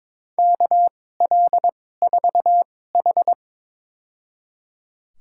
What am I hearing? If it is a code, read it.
Morse code, heard as a keyed tone, about 23 wpm, spaced as for 16 wpm, 700 Hz. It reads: KL4H
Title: USA Call Signs - 4 Characters - 2x Character Spacing - 22wpm